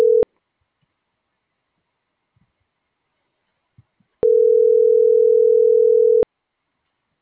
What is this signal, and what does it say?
Ringback tone (North America), 440 + 480 Hz.